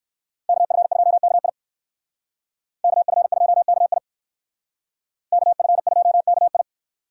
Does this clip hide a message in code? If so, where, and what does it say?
Yes: Morse code from 0.5 to 6.6 s, heard as a keyed tone, about 59 wpm, 700 Hz. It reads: KF2KI KF2KI KF2KI